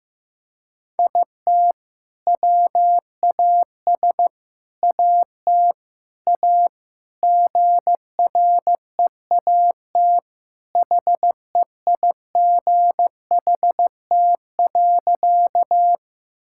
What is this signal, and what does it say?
Morse code, heard as a keyed tone, about 16 wpm, 700 Hz. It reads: IT WAS AT A GREAT HEIGHT.
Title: Sentences from Top 1600 Words 15wpm